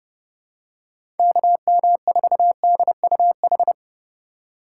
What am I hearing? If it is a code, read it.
Morse code, heard as a keyed tone, about 29 wpm, 700 Hz. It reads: KM4DUH